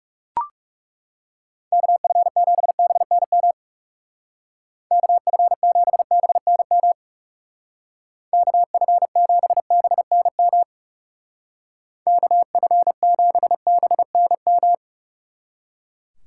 Morse code, heard as a keyed tone, about 38 wpm, 700 Hz. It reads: KF7BNM KF7BNM KF7BNM KF7BNM